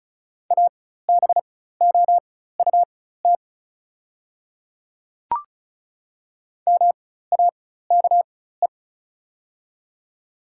Morse code, heard as a keyed tone, about 34 wpm, spaced as for 16 wpm, 690 Hz. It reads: ABOUT MAKE
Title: 4x Character Spacing - Top 100 Words - 35wpm